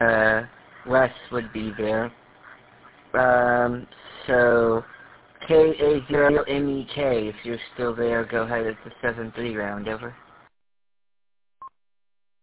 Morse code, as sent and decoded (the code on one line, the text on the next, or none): none